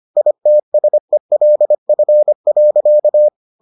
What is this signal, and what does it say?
Morse code, heard as a keyed tone, about 25 wpm, 600 Hz. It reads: ITSELF.